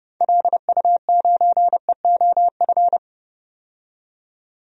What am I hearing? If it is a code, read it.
Morse code, heard as a keyed tone, about 30 wpm, 700 Hz. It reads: LU9EOF
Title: DXCC-100 Call Signs - 6 Characters - 30wpm